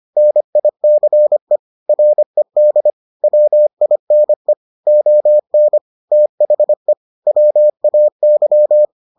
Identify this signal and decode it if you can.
Morse code, heard as a keyed tone, about 25 wpm, 600 Hz. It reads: NICE RED WINE ON THE WAY